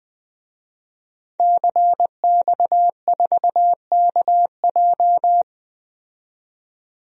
Morse code, heard as a keyed tone, about 20 wpm, 700 Hz. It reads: CX4KJ
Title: DXCC-100 Call Signs - 5 Characters - 20wpm